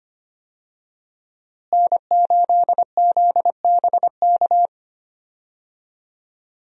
Morse code, heard as a keyed tone, about 25 wpm, 700 Hz. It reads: N8ZBK
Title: USA Call Signs - 5 Characters - Repeated 3x - 25wpm